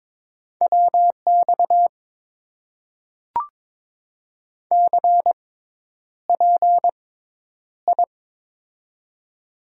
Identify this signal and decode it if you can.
Morse code, heard as a keyed tone, about 23 wpm, spaced as for 10 wpm, 700 Hz. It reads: WX CPI
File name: Top 90 QSO Elements - 6x Character Spacing - 22wpm